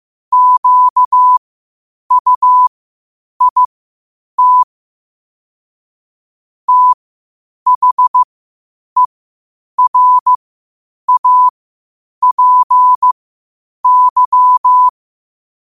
Morse code, heard as a keyed tone, about 15 wpm, spaced as for 8 wpm, 1000 Hz. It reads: QUIT THERAPY